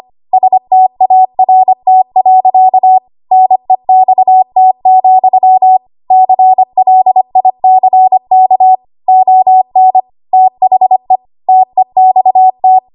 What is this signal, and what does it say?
Morse code, heard as a keyed tone, about 25 wpm, 750 Hz. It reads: START. NEXT, CLICK ON THE TEXT
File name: Archive / 25 WPM / 25WPM